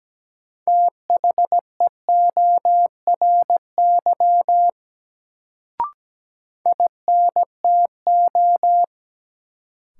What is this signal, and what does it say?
Morse code, heard as a keyed tone, about 17 wpm, 700 Hz. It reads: THEORY INTO
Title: Sentences from Top 600 Words - New Word Drill - 17wpm